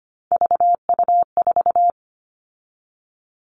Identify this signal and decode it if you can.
Morse code, heard as a keyed tone, about 25 wpm, 700 Hz. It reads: VU4
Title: All Cut Numbers - 3-Digit 25wpm